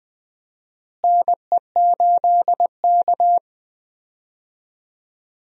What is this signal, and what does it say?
Morse code, heard as a keyed tone, about 20 wpm, 700 Hz. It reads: NE8K
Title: USA Call Signs - 4 Characters - Repeated 3x - 20wpm